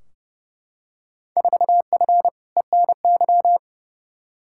Morse code, heard as a keyed tone, about 30 wpm, 700 Hz. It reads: 4F ENY
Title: Field Day Exchanges - 30wpm